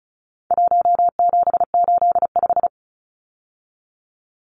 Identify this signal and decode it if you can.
Morse code, heard as a keyed tone, about 34 wpm, 700 Hz. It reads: 1785
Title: Numbers - 4-Digit 35wpm